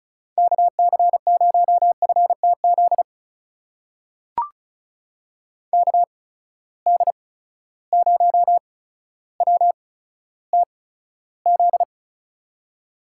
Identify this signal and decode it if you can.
Morse code, heard as a keyed tone, about 36 wpm, spaced as for 15 wpm, 700 Hz. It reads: KC0FTZ KD0WTZ
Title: USA Call Signs - 6 Characters - 8x Character Spacing - 35wpm